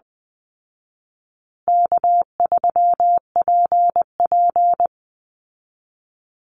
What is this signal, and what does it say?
Morse code, heard as a keyed tone, about 20 wpm, 700 Hz. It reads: K3PP
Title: Easy Call Signs Repeated 3x - 20wpm